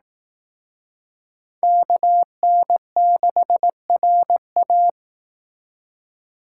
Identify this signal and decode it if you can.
Morse code, heard as a keyed tone, about 18 wpm, 700 Hz. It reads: KN6RA
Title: Speed Racing - USA 2x2 Call Signs 15wpm